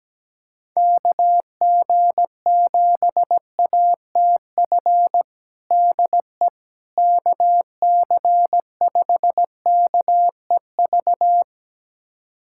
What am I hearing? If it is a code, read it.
Morse code, heard as a keyed tone, about 17 wpm, 700 Hz. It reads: KG7ATF DE KC5KEV
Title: USA Callsign DE Callsign - 6 Characters - 17wpm